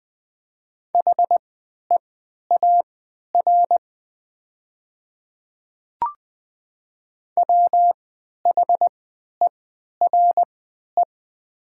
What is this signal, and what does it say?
Morse code, heard as a keyed tone, about 20 wpm, spaced as for 11 wpm, 700 Hz. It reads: HEAR WHERE
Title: 3x Character Spacing - Top 300 Words - 20wpm